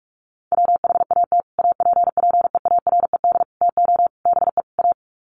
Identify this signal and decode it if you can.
Morse code, heard as a keyed tone, about 44 wpm, 700 Hz. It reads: WHAT APPEARED TO BE A
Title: Sets of 5 Words Encore2 45wpm